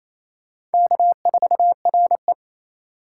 Morse code, heard as a keyed tone, about 28 wpm, 700 Hz. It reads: K4RE
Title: USA Call Signs - 4 Characters - 4x Character Spacing - 28wpm